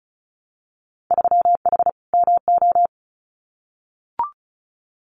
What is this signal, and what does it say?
Morse code, heard as a keyed tone, about 34 wpm, 700 Hz. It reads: 3H MO
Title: Winter Field Day Exchanges Encore 35wpm